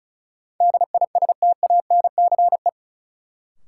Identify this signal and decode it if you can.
Morse code, heard as a keyed tone, about 36 wpm, 700 Hz. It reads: DISTANCE
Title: Sentences from Top 1200 Words - New Word Drill Alt - 35wpm